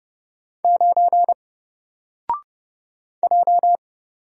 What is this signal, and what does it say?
Morse code, heard as a keyed tone, about 30 wpm, 700 Hz. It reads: 9 J
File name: Single-Letter-Number-30wpm